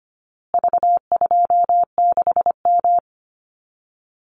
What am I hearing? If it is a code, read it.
Morse code, heard as a keyed tone, about 25 wpm, 700 Hz. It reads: V26M
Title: Call Signs Repeated 2x - 25wpm